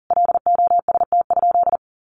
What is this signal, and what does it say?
Morse code, heard as a keyed tone, about 40 wpm, 700 Hz. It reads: LOST?
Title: Sentences from Top 1400 Words 40wpm